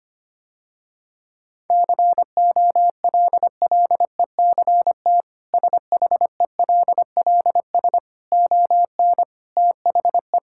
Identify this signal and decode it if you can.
Morse code, heard as a keyed tone, about 25 wpm, 700 Hz. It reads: COLLECT SHELLS ON THE
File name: Easy Sentences with Hints - 25wpm